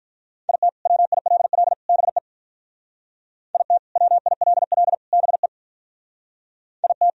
Morse code, heard as a keyed tone, about 44 wpm, 700 Hz. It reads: IT WILL BE IT WILL BE IT